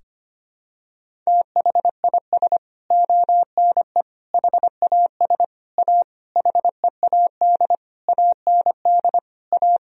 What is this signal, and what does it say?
Morse code, heard as a keyed tone, about 25 wpm, 700 Hz. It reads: THIS ONE HAS A HEAD AND A
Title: Sentences from Top 1000 Words - Review Part 3 - 25wpm